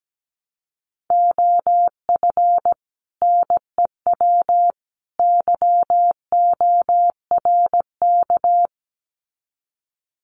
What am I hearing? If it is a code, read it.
Morse code, heard as a keyed tone, about 17 wpm, 700 Hz. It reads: OF NEW YORK
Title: Sets of 3 Words 17wpm